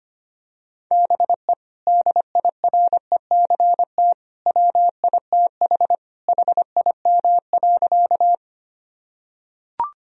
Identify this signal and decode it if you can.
Morse code, heard as a keyed tone, about 25 wpm, 700 Hz. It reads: BE DIRECT WITH HIM.
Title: Sentences from Top 1000 Words - Review Part 2 - 25wpm